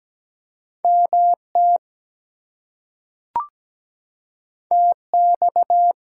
Morse code, heard as a keyed tone, about 17 wpm, 700 Hz. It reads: MT TX
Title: US States and Canadian Provinces 17wpm